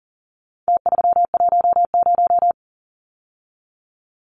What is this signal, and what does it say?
Morse code, heard as a keyed tone, about 40 wpm, 700 Hz. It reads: T310
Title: Numbers - Contest Style - 4-Digit 40wpm